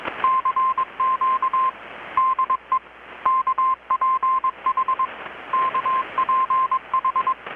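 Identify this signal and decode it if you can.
Morse code, heard as a keyed tone, about 21 wpm, 1030 Hz. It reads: CQ DE KPH KPH
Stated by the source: Morse from KPH's Night of Nights XX maritime radiotelegraphy event, received in Chiapas, Mexico about 3,800 km away on a Kenwood R-600 with a grounded 10 m long-wire antenna.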